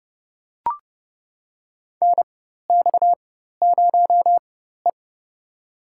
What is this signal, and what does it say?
Morse code, heard as a keyed tone, about 30 wpm, spaced as for 14 wpm, 700 Hz. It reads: NX0E
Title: USA Call Signs - 4 Characters - 4x Character Spacing - 30wpm